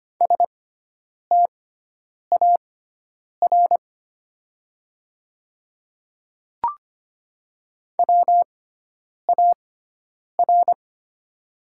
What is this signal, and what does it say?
Morse code, heard as a keyed tone, about 25 wpm, spaced as for 9 wpm, 700 Hz. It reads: STAR WAR
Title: 6x Character Spacing - Top 500 Words - 25wpm